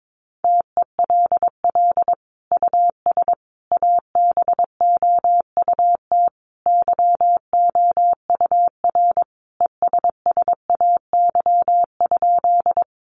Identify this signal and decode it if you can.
Morse code, heard as a keyed tone, about 22 wpm, 700 Hz. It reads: TELL US ABOUT YOUR ESSAY?
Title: Sentences from Top 1500 Words 22wpm